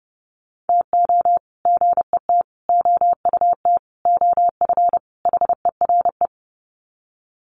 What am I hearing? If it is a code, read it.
Morse code, heard as a keyed tone, about 30 wpm, 700 Hz. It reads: TO GET OUT OF HERE
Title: Sets of 5 Words Encore 30wpm